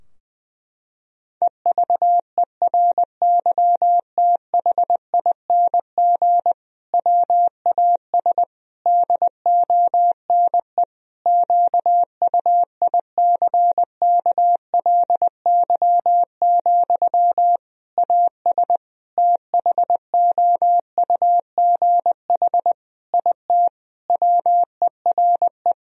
Morse code, heard as a keyed tone, about 21 wpm, 700 Hz. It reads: EVERYTHING WAS DONE QUICKLY, AS THOUGH IT WERE
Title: Sentences from Top 1000 Words - Review Part 2 - 20wpm